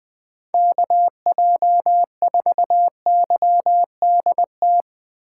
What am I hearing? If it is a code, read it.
Morse code, heard as a keyed tone, about 20 wpm, 700 Hz. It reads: KJ4YDT